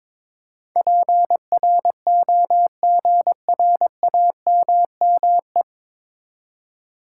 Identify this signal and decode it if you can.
Morse code, heard as a keyed tone, about 21 wpm, 700 Hz. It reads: PROGRAMME
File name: 9 Letter Words - First 3 Letters Given - 22wpm